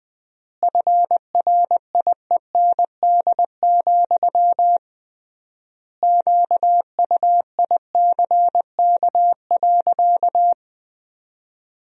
Morse code, heard as a keyed tone, about 20 wpm, 700 Hz. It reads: FRIEND, QUICK.
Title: 3x Word-Spacing - Sentences from Top 500 Words - 20wpm